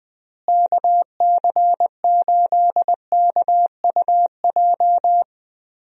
Morse code, heard as a keyed tone, about 20 wpm, 700 Hz. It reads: KC8KUJ
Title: USA Call Signs - 6 Characters - Repeated 3x - 20wpm